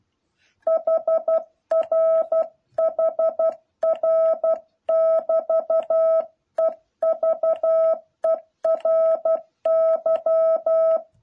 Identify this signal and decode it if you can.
Morse code, heard as a keyed tone, about 12 wpm, 650 Hz. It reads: HR HR<BT>EVERY
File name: jh8cbh / 2021cw